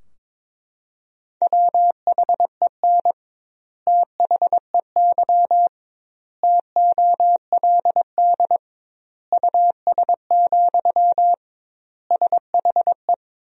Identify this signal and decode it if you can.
Morse code, heard as a keyed tone, about 22 wpm, 700 Hz. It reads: WHEN THEY TOLD US, SHE
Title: 2x Word-Spacing - Sentences from Top 400 Words - 22wpm